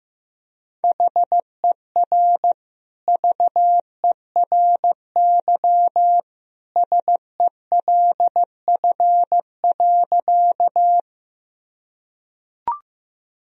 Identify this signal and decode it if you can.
Morse code, heard as a keyed tone, about 15 wpm, 700 Hz. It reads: HER VERY SELF.
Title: Sentences from Top 1000 Words - Review Part 1 - 15wpm